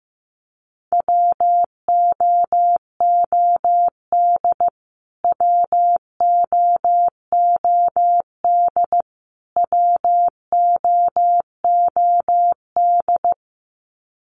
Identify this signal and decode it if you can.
Morse code, heard as a keyed tone, about 15 wpm, 700 Hz. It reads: WOOD WOOD WOOD
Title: Top 300 Words Repeated 15wpm